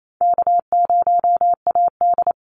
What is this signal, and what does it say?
Morse code, heard as a keyed tone, about 28 wpm, 700 Hz. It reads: K0AD
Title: Call Signs Repeated 3x - 28wpm